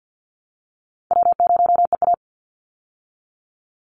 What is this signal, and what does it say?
Morse code, heard as a keyed tone, about 50 wpm, 700 Hz. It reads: W0EA
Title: Call Signs Repeated 2x - 50wpm